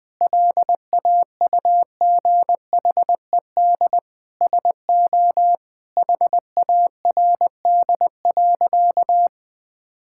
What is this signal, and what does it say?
Morse code, heard as a keyed tone, about 20 wpm, 700 Hz. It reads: LAUGHED SO HARD.
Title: Sentences with 7-Word Phrases - 20wpm